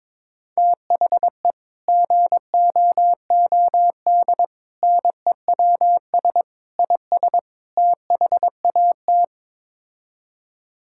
Morse code, heard as a keyed tone, about 23 wpm, 700 Hz. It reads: THE GOOD NEWS IS THAT
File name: practice / 22wpm / Build A Phrase - 5-Word Phrases - 22wpm